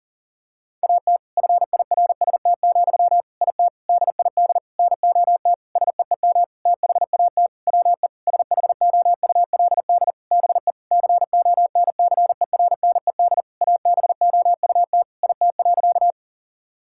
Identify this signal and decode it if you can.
Morse code, heard as a keyed tone, about 37 wpm, 700 Hz. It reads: AT FIRST, IT DID NOT SEEM THAT WE SHOULD BE CONCERNED ABOUT IT.
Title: Sentences from Top 1000 Words - Review Part 6 - 40wpm